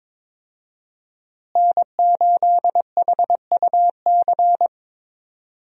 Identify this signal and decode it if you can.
Morse code, heard as a keyed tone, about 22 wpm, 700 Hz. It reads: N8HUC
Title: USA Call Signs - 5 Characters - Repeated 3x - 22wpm